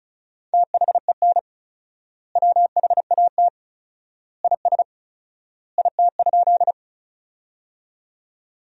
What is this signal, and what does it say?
Morse code, heard as a keyed tone, about 34 wpm, 700 Hz. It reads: THEN WHAT IS IT?
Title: 4x Word-Spacing - Sentences from Top 100 Words - 35wpm